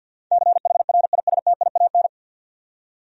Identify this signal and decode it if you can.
Morse code, heard as a keyed tone, about 50 wpm, 700 Hz. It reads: CHRISTIAN
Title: USA Common Names - 50wpm